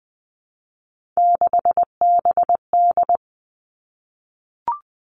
Morse code, heard as a keyed tone, about 20 wpm, 700 Hz. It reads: 6BD
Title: Three Challenging Letter-Number Combinations 20wpm